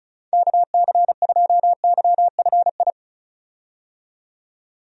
Morse code, heard as a keyed tone, about 36 wpm, 700 Hz. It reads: KC2YFI
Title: USA Call Signs - 6 Characters - 35wpm